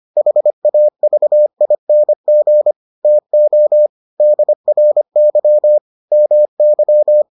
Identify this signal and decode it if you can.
Morse code, heard as a keyed tone, about 25 wpm, 600 Hz. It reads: HAVING TO DRY MY